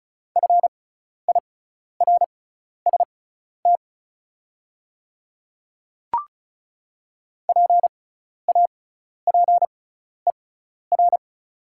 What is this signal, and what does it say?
Morse code, heard as a keyed tone, about 34 wpm, spaced as for 12 wpm, 700 Hz. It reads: FIRST PAPER
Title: 6x Character Spacing - Top 300 Words - 35wpm